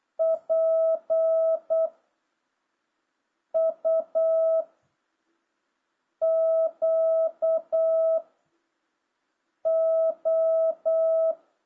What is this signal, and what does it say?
Morse code, heard as a keyed tone, about 8 wpm, 650 Hz. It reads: P U Q O